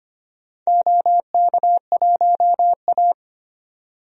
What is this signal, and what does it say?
Morse code, heard as a keyed tone, about 25 wpm, 700 Hz. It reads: OK1A
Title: DXCC-100 Call Signs - 4 Characters - 25wpm